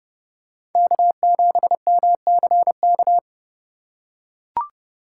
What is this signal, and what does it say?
Morse code, heard as a keyed tone, about 30 wpm, 700 Hz. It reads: K7MCK
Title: USA Call Signs - 5 Characters - 4x Character Spacing - 30wpm